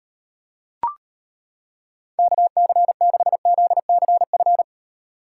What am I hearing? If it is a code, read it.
Morse code, heard as a keyed tone, about 31 wpm, 700 Hz. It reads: KC6ZCF
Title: Speed Racing - USA 2x3 Call Signs 25wpm